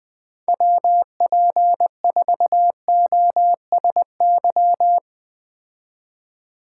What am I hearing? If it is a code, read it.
Morse code, heard as a keyed tone, about 20 wpm, 700 Hz. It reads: WP4OSY